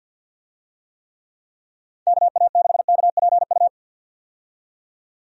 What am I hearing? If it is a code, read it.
Morse code, heard as a keyed tone, about 46 wpm, 700 Hz. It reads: KA6KPU
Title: USA Call Signs - 6 Characters - 50wpm